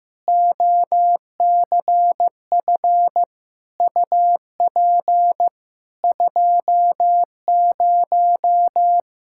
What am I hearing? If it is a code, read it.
Morse code, heard as a keyed tone, about 15 wpm, 700 Hz. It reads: OCF UP 20
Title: K1USN SST QSO Practice 15wpm